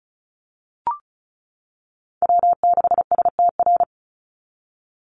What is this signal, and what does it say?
Morse code, heard as a keyed tone, about 34 wpm, 700 Hz. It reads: W6STR
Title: NA SOTA Chaser Callsigns 35wpm